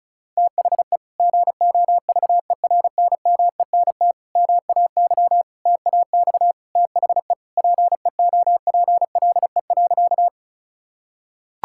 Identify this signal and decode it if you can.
Morse code, heard as a keyed tone, about 36 wpm, 700 Hz. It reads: THE GOVERNMENT MAY TAX THE PEOPLE.
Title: Sentences from Top 1000 Words - Review Part 5 - 35wpm